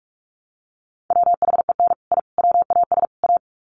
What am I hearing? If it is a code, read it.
Morse code, heard as a keyed tone, about 44 wpm, 700 Hz. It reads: WHEN I WAS A
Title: Sets of 4 Words 45wpm